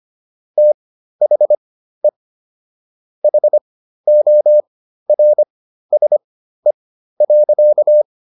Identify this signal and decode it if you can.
Morse code, heard as a keyed tone, about 25 wpm, spaced as for 13 wpm, 600 Hz. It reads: THE HORSE.